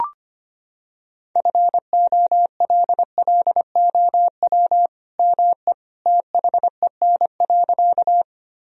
Morse code, heard as a keyed tone, about 25 wpm, 700 Hz. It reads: FOLLOW ME THEN.
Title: Sentences from Top 1000 Words - Review Part 1 - 25wpm